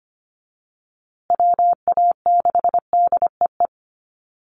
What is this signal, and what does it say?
Morse code, heard as a keyed tone, about 25 wpm, 700 Hz. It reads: WA6DEE
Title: USA Call Signs Repeated 2x 25wpm